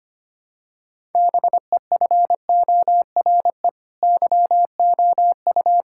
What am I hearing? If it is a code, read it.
Morse code, heard as a keyed tone, about 25 wpm, 700 Hz. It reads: BEFORE YOU
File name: Sets of 9 Words - 25wpm